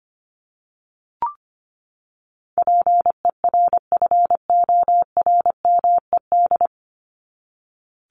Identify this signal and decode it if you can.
Morse code, heard as a keyed tone, about 25 wpm, 700 Hz. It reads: PERFORMED
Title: ED Suffix - Top 100 Words with Base Words 25wpm